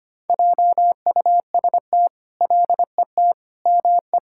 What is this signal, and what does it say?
Morse code, heard as a keyed tone, about 25 wpm, 700 Hz. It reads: JUST LET ME
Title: Sentences from Top 1000 Words - Review Part 5 - 25wpm